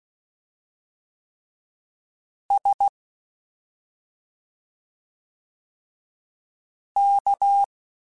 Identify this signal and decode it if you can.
Morse code, heard as a keyed tone, about 16 wpm, 800 Hz. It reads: S K
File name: Special INGs-mp3s / Special INGs-mp3s / Slow ing @10